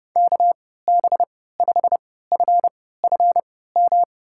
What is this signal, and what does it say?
Morse code, heard as a keyed tone, about 32 wpm, spaced as for 17 wpm, 700 Hz. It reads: KB5FFM
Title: USA Call Signs - 6 Characters - 3x Character Spacing - 30wpm